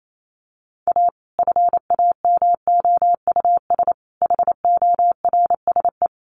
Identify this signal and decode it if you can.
Morse code, heard as a keyed tone, about 28 wpm, 700 Hz. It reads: A FAMOUS HORSE
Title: Sentences from Top 1500 Words 28wpm